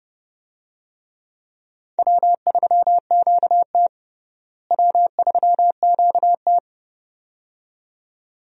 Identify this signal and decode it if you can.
Morse code, heard as a keyed tone, about 30 wpm, 700 Hz. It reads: W3QT W3QT
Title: USA Call Signs - 4 Characters - Repeated 2x - 30wpm